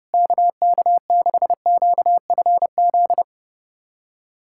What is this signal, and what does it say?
Morse code, heard as a keyed tone, about 30 wpm, 700 Hz. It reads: KK6QFZ
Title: USA Callsign DE Callsign 30wpm